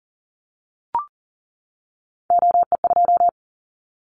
Morse code, heard as a keyed tone, about 40 wpm, 700 Hz. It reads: OE2
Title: Callsign Prefixes for Top 50 DXCC - 40wpm